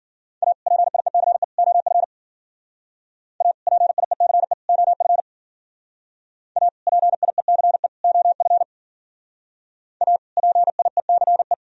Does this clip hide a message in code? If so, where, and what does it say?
Yes: Morse code from 0.4 to 11.5 s, heard as a keyed tone, about 53 wpm, 700 Hz. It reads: A PIECE OF A PIECE OF A PIECE OF A PIECE